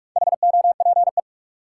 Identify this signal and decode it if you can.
Morse code, heard as a keyed tone, about 43 wpm, 700 Hz. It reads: HOPE
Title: Sentences from Top 1000 Words - New Word Drill - 45wpm